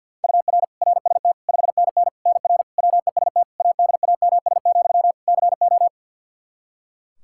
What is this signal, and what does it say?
Morse code, heard as a keyed tone, about 51 wpm, 700 Hz. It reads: UR RST 5NN NR WEST ADAMS, CO
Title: Practice QSO - Exchange 1 - Beginning - 50wpm